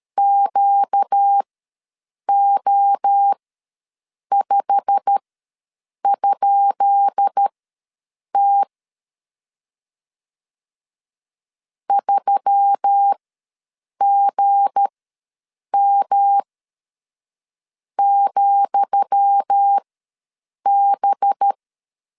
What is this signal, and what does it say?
Morse code, heard as a keyed tone, about 13 wpm, spaced as for 7 wpm, 800 Hz. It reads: QO5?T 3GM,B
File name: LEZIONI / 10 SIGNS / 10D COMA ws 12wpm 3618